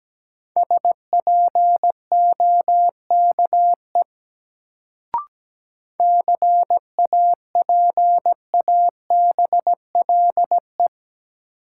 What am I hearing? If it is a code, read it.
Morse code, heard as a keyed tone, about 17 wpm, 700 Hz. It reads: SPOKE CAPABLE
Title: Sentences from Top 1200 Words - New Word Drill Alt - 17wpm